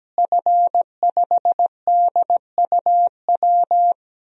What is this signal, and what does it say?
Morse code, heard as a keyed tone, about 17 wpm, 700 Hz. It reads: F5DUW
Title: DXCC-100 Call Signs - 5 Characters - Repeated 2x - 17wpm